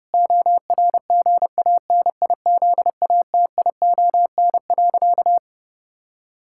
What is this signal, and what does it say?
Morse code, heard as a keyed tone, about 32 wpm, 700 Hz. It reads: ORGANIZATION.
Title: Sentences from Top 1000 Words - Review Part 6 - 30wpm